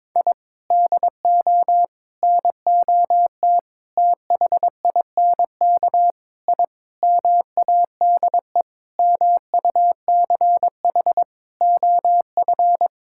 Morse code, heard as a keyed tone, about 22 wpm, 700 Hz. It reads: I DO NOT THINK I MADE MUCH OF
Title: Sentences from Top 1000 Words - Review Part 6 - 22wpm